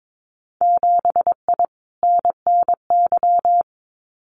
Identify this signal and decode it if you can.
Morse code, heard as a keyed tone, about 22 wpm, 700 Hz. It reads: 7I NNY
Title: Winter Field Day Exchanges Encore 22wpm